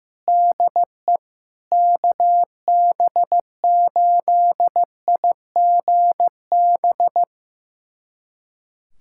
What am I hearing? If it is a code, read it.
Morse code, heard as a keyed tone, about 15 wpm, 700 Hz. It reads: DE KB8IGB
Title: USA Callsign DE Callsign - 6 Characters - 15wpm